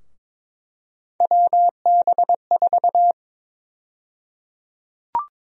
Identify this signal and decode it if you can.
Morse code, heard as a keyed tone, about 22 wpm, 700 Hz. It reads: WB4
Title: USA Call Sign Prefixes - 22wpm